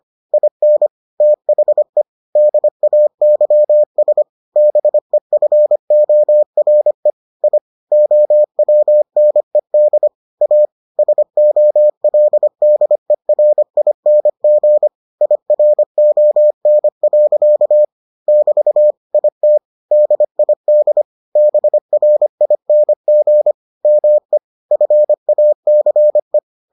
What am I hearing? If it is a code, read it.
Morse code, heard as a keyed tone, about 25 wpm, 600 Hz. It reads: IN THE DAYS BEFORE I OWNED A SOLDERING IRON. <BT> IT DID BRING ME FACE